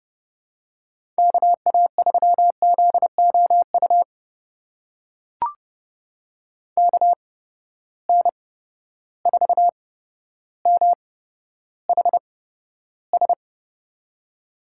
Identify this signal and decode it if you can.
Morse code, heard as a keyed tone, about 30 wpm, spaced as for 13 wpm, 700 Hz. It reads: KA3ZOU KN4MHS
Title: USA Call Signs - 6 Characters - 8x Character Spacing - 30wpm